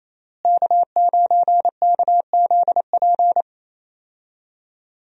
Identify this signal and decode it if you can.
Morse code, heard as a keyed tone, about 27 wpm, 700 Hz. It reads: K9KZP
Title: USA Call Signs - 5 Characters - Repeated 3x - 28wpm